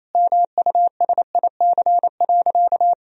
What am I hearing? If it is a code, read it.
Morse code, heard as a keyed tone, about 28 wpm, 700 Hz. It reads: MUSIC.
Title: Sentences from Top 1000 Words - Review Part 1 - 28wpm